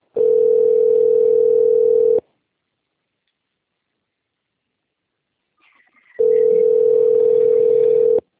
Ringback tone (North America), 440 + 480 Hz.